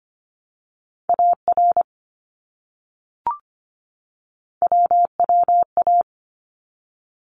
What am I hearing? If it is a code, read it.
Morse code, heard as a keyed tone, about 25 wpm, 700 Hz. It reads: AR WWA
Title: ARRL Sections 25wpm